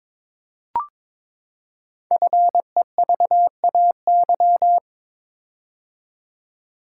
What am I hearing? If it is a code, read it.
Morse code, heard as a keyed tone, about 22 wpm, 700 Hz. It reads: FEVAY
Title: Code Groups - 5 Letter:Numbers - 22wpm